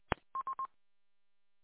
There is no code.